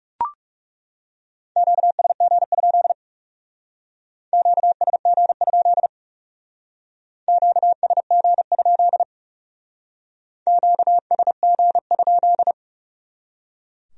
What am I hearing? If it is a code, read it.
Morse code, heard as a keyed tone, about 38 wpm, 700 Hz. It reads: QSG? QSG? QSG? QSG?